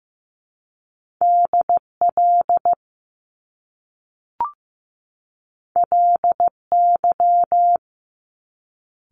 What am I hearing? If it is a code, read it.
Morse code, heard as a keyed tone, about 15 wpm, 700 Hz. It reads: DL LY
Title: Top Two Letter Combinations in Words 15wpm